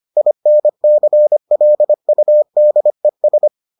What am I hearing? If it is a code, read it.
Morse code, heard as a keyed tone, about 25 wpm, 600 Hz. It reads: INCLUDES